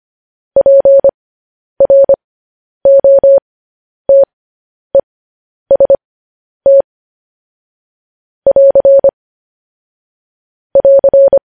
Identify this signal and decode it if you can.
Morse code, heard as a keyed tone, about 25 wpm, spaced as for 10 wpm, 550 Hz. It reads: PROTEST <AR> <AR>
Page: News Headlines in Morse Code at 10 WPM